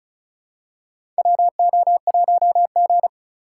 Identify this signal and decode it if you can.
Morse code, heard as a keyed tone, about 34 wpm, 700 Hz. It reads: WO1G
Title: USA Call Signs - 4 Characters - 6x Character Spacing - 35wpm